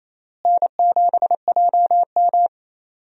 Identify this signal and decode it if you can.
Morse code, heard as a keyed tone, about 28 wpm, 700 Hz. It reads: N7JM
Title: USA Call Signs - 4 Characters - Repeated 3x - 28wpm